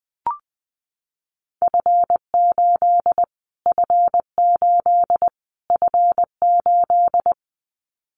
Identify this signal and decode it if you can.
Morse code, heard as a keyed tone, about 20 wpm, 700 Hz. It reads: F8 F8 F8
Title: Callsign Prefixes for Top 25 DXCC - Repeated 3x - 20wpm